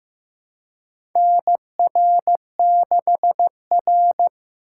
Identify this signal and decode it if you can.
Morse code, heard as a keyed tone, about 15 wpm, 700 Hz. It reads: NR6R
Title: USA Call Signs - 4 Characters - 2x Character Spacing - 15wpm